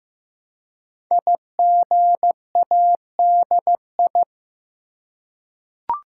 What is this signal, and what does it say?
Morse code, heard as a keyed tone, about 15 wpm, 700 Hz. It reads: IGADI
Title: Code Groups - 5 Letters - 15wpm